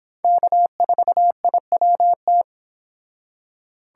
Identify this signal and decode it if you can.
Morse code, heard as a keyed tone, about 32 wpm, 700 Hz. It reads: K4IWT